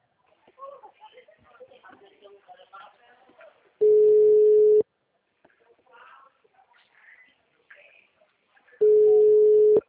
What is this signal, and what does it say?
Ringback tone (Europe, 425 Hz), with speech.